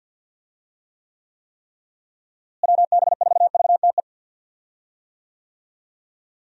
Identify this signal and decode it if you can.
Morse code, heard as a keyed tone, about 52 wpm, 700 Hz. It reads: WB4VTE
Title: USA Call Signs - 6 Characters - Repeated 3x - 50wpm